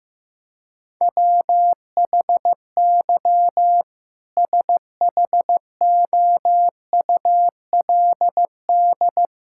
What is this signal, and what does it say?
Morse code, heard as a keyed tone, about 15 wpm, 700 Hz. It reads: WHY SHOULD
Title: Sentences from Top 1600 Words 15wpm